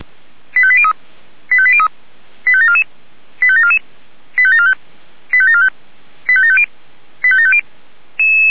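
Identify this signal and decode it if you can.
Five-tone selective call (ZVEI), heard as quick runs of five tones, tones of about 70 ms. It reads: 85803 85803 85744 85744 85745 85745 85750 85750
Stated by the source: ZVEI five-tone selective call received on SDR, 464 channel path.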